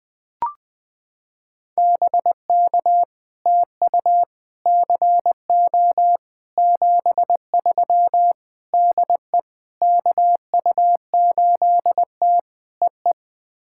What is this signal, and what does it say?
Morse code, heard as a keyed tone, about 21 wpm, 700 Hz. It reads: BK TU CO 73 DE KU8T EE
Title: POTA QSO Practice - 20wpm